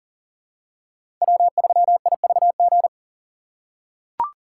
Morse code, heard as a keyed tone, about 40 wpm, 700 Hz. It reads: W3IVG
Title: USA Call Signs - 5 Characters - 6x Character Spacing - 40wpm